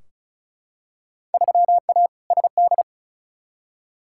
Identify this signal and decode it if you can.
Morse code, heard as a keyed tone, about 34 wpm, 700 Hz. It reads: 3A SD
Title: Field Day Exchanges - 35wpm